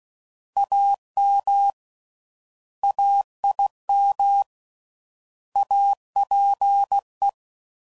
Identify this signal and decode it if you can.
Morse code, heard as a keyed tone, about 16 wpm, 800 Hz. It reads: AM AIM APE
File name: Prefixes / By First Letter / As